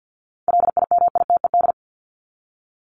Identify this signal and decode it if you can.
Morse code, heard as a keyed tone, about 50 wpm, 700 Hz. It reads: LIMITED
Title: Top 751-1000 Words 50wpm